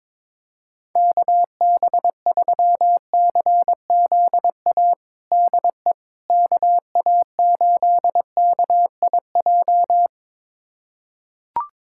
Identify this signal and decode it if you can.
Morse code, heard as a keyed tone, about 22 wpm, 700 Hz. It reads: KB3CZA DE KA8KIJ
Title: USA Callsign DE Callsign - 6 Characters - 22wpm